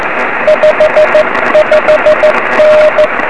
Morse code, heard as a keyed tone, about 15 wpm, 590 Hz, with radio noise. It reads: 55N
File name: 2108z pa3awj 23cm